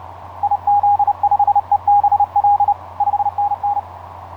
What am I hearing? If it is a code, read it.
Morse code, heard as a keyed tone, about 30 wpm, 820 Hz. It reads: IZ5EBL 5NN
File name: 5r8x iz5ebl etc 12m cw 07nov2010 1003